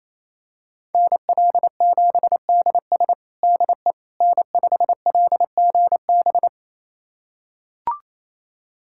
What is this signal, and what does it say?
Morse code, heard as a keyed tone, about 28 wpm, 700 Hz. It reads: NL7DS DE N5LGB
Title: USA Callsign DE Callsign - 5 Characters - 28wpm